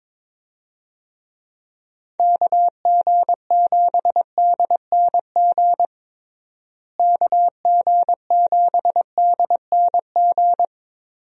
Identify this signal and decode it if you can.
Morse code, heard as a keyed tone, about 22 wpm, 700 Hz. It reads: KG7DNG KG7DNG
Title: USA Call Signs - 6 Characters - Repeated 2x - 22wpm